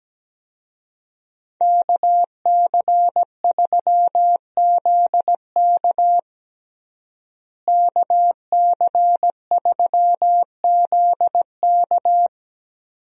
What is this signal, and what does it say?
Morse code, heard as a keyed tone, about 17 wpm, 700 Hz. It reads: KC3ZK KC3ZK
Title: USA Call Signs - 5 Characters - Repeated 2x - 17wpm